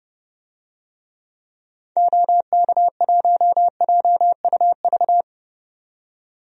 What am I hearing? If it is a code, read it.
Morse code, heard as a keyed tone, about 31 wpm, 700 Hz. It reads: OK1JUV
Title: DXCC-100 Call Signs - 6 Characters - 30wpm